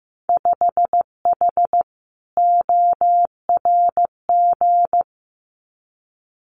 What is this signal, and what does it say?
Morse code, heard as a keyed tone, about 15 wpm, 700 Hz. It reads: 5H ORG
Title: Winter Field Day Exchanges Encore 15wpm